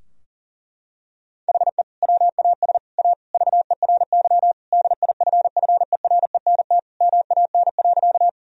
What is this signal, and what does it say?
Morse code, heard as a keyed tone, about 40 wpm, 700 Hz. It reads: HE WAS A VERY DIFFERENT MAN.
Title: Sentences from Top 1000 Words - Review Part 3 - 40wpm